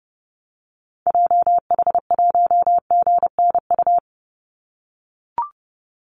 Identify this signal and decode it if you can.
Morse code, heard as a keyed tone, about 32 wpm, 700 Hz. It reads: JH1GNU
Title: Call Signs - 30wpm